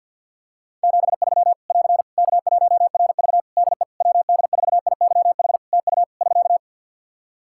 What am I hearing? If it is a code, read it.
Morse code, heard as a keyed tone, about 47 wpm, 700 Hz. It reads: 73 <AR> K1RV DE WD4IYH TU <SK>
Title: Practice QSO - Exchange 4 - Ending - 50wpm